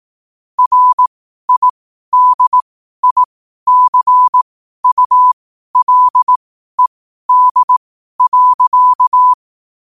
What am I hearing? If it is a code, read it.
Morse code, heard as a keyed tone, about 18 wpm, spaced as for 12 wpm, 1000 Hz. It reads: RIDICULED.
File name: qotdfiles / 12wpm